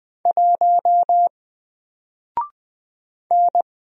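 Morse code, heard as a keyed tone, about 20 wpm, 700 Hz. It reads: 1 N
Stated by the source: Single letter and number Morse drill at 20 wpm, from N4LCD.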